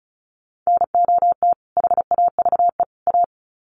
Morse code, heard as a keyed tone, about 34 wpm, 700 Hz. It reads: NOT HAVE A
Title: Sets of 3 Words 35wpm